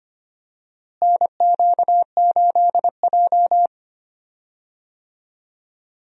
Morse code, heard as a keyed tone, about 25 wpm, 700 Hz. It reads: NQ8J
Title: USA Call Signs - 4 Characters - Repeated 2x - 25wpm